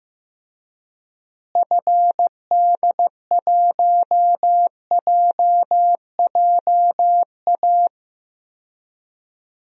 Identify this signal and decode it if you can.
Morse code, heard as a keyed tone, about 15 wpm, 700 Hz. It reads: FD1JJA
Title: DXCC-100 Call Signs - 6 Characters - Repeated 3x - 15wpm